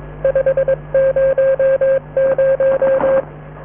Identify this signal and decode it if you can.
Morse code, heard as a keyed tone, about 22 wpm, 540 Hz, with radio noise. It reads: 500